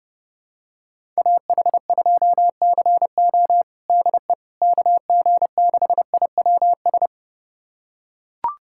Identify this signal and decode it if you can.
Morse code, heard as a keyed tone, about 30 wpm, 700 Hz. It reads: AH2CO DE KG6IWS
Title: USA Callsign DE Callsign 30wpm